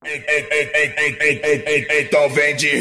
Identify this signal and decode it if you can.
There is no code.